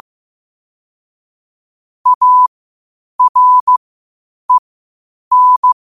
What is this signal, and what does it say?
Morse code, heard as a keyed tone, about 15 wpm, spaced as for 8 wpm, 1000 Hz. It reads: AREN
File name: qotdfiles / 8wpm